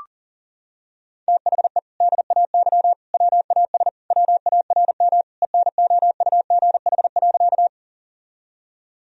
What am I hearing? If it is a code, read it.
Morse code, heard as a keyed tone, about 40 wpm, 700 Hz. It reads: THE DAY WAS WARM ENOUGH.
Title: Sentences from Top 1000 Words - Review Part 2 - 40wpm